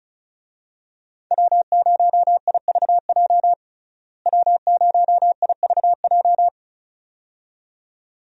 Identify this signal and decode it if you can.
Morse code, heard as a keyed tone, about 36 wpm, 700 Hz. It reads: W0IVJ W0IVJ
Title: USA Call Signs - 5 Characters - Repeated 2x - 35wpm